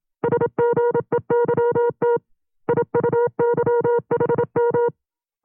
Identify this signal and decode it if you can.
Morse code, heard as a keyed tone, about 27 wpm, 480 Hz. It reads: SGEYT IUYHM
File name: CW-practice/25WPM